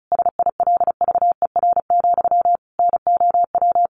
Morse code, heard as a keyed tone, about 34 wpm, 700 Hz. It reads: SILVER, NOW
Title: A Sound Of Thunder 35wpm